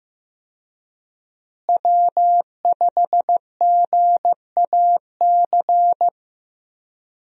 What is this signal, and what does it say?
Morse code, heard as a keyed tone, about 15 wpm, 700 Hz. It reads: W5GAC
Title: USA Call Signs - 5 Characters - Repeated 2x - 15wpm